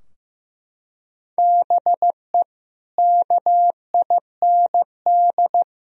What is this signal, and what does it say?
Morse code, heard as a keyed tone, about 15 wpm, 700 Hz. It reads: BE KIND
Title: Sentences from Top 1000 Words - Review Part 2 - 15wpm